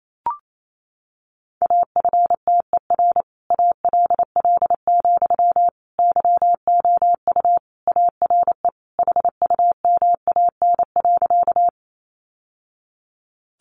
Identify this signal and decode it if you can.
Morse code, heard as a keyed tone, about 28 wpm, 700 Hz. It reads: AFTER ALL, YOU ARE HUMAN.
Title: Sentences from Top 600 Words 28wpm